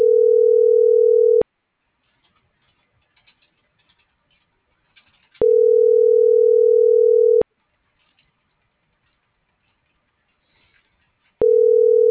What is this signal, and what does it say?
Ringback tone (North America), 440 + 480 Hz.
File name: CENTRAL SOTOVA / out alejandro.rodriguez - .56068